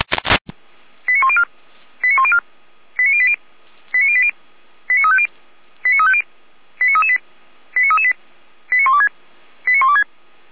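Five-tone selective call (ZVEI), heard as quick runs of five tones, tones of about 70 ms. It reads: 89104 89104 89080 89080 89350 89350 89338 89338 89126 89126
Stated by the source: ZVEI five-tone selective call received with an SDR.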